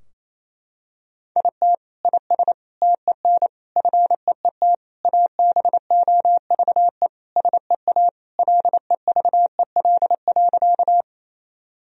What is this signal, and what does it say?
Morse code, heard as a keyed tone, about 28 wpm, 700 Hz. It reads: IT IS TEN FEET ABOVE SEA LEVEL.
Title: Sentences from Top 1000 Words - Review Part 4 - 28wpm